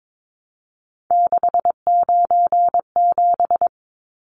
Morse code, heard as a keyed tone, about 22 wpm, 700 Hz. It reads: 697
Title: Numbers - 3-Digit 22wpm